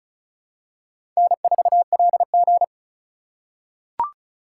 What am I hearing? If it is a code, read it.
Morse code, heard as a keyed tone, about 36 wpm, 700 Hz. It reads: N4LG
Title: USA Call Signs - 4 Characters - 4x Character Spacing - 35wpm